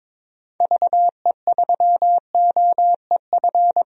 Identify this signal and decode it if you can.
Morse code, heard as a keyed tone, about 21 wpm, 700 Hz. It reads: VE3OEF
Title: DXCC-100 Call Signs - 6 Characters - 22wpm